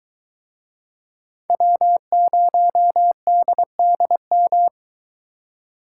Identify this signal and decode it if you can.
Morse code, heard as a keyed tone, about 23 wpm, 700 Hz. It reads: W0DDM